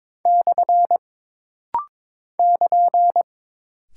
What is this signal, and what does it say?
Morse code, heard as a keyed tone, about 22 wpm, 700 Hz. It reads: / <KN>
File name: practice / 22wpm / Single Punctuation-Prosign - Rapid-Fire - 22wpm